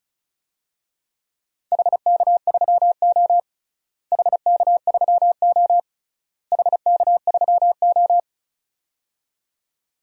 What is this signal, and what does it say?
Morse code, heard as a keyed tone, about 36 wpm, 700 Hz. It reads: HK3O HK3O HK3O
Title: DXCC-100 Call Signs - 4 Characters - Repeated 3x - 35wpm